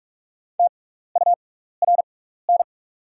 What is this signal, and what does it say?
Morse code, heard as a keyed tone, about 43 wpm, spaced as for 15 wpm, 690 Hz. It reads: TURN
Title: 6x Character Spacing - Top 200 Words - 45wpm